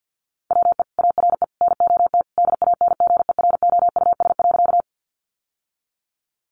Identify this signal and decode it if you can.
Morse code, heard as a keyed tone, about 50 wpm, 700 Hz. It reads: WE ARE NOT DANGEROUS.
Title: Sentences from Top 1200 Words 50wpm